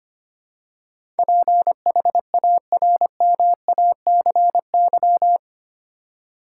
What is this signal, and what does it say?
Morse code, heard as a keyed tone, about 25 wpm, 700 Hz. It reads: PHARMACY
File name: 8 Letter Words - First 3 Letters Given - 25wpm